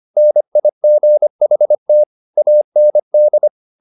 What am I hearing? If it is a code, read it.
Morse code, heard as a keyed tone, about 25 wpm, 600 Hz. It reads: NIGHT AND